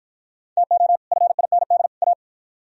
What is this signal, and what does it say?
Morse code, heard as a keyed tone, about 52 wpm, 700 Hz. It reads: TO FIND A